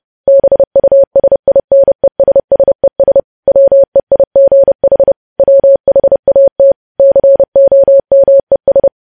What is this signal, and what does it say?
Morse code, heard as a keyed tone, about 30 wpm, 550 Hz. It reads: BUSINESSES WEIGH WHAT COMES
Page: news headlines in morse code at 30 wpm